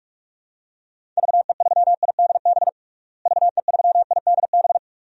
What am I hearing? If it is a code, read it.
Morse code, heard as a keyed tone, about 46 wpm, 700 Hz. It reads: VE3IDB VE3IDB
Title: DXCC-100 Call Signs - 6 Characters - Repeated 2x - 45wpm